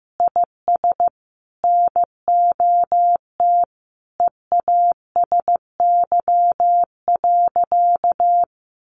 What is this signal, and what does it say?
Morse code, heard as a keyed tone, about 15 wpm, 700 Hz. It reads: IS NOT EASY.
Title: Sentences from Top 1500 Words 15wpm